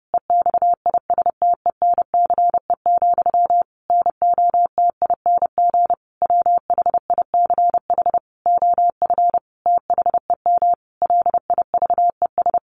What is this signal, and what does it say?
Morse code, heard as a keyed tone, about 30 wpm, 700 Hz. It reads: EXISTENCE, NOTING WHICH OF THEM LIVES
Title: A Sound Of Thunder 30wpm